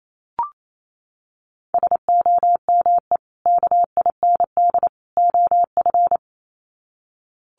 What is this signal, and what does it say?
Morse code, heard as a keyed tone, about 28 wpm, 700 Hz. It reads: SOME KIND OF
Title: Sets of 3 Words 28wpm